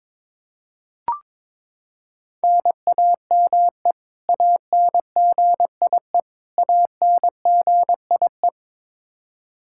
Morse code, heard as a keyed tone, about 22 wpm, 700 Hz. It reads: NAME ANGIE ANGIE
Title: Practice QSO - Exchange 1 - Beginning - 22wpm